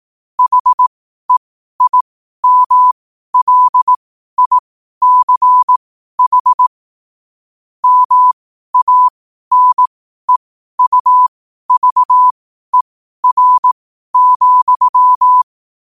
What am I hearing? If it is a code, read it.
Morse code, heard as a keyed tone, about 18 wpm, spaced as for 12 wpm, 1000 Hz. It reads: HEIMLICH MANEUVER,
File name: qotdfiles / 12wpm